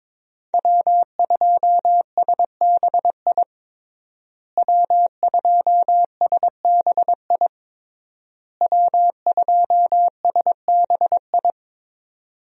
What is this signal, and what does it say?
Morse code, heard as a keyed tone, about 22 wpm, 700 Hz. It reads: W2SBI W2SBI W2SBI
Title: USA Call Signs - 5 Characters - Repeated 3x - 22wpm